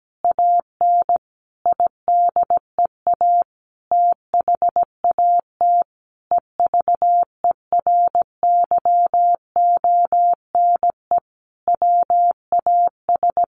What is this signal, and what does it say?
Morse code, heard as a keyed tone, about 17 wpm, 700 Hz. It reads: AN IDEA THAT EVERYONE WAS
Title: Sentences from Top 1400 Words 17wpm